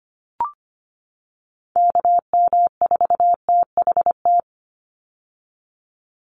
Morse code, heard as a keyed tone, about 25 wpm, 700 Hz. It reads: KM4THT
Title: USA Call Signs 25wpm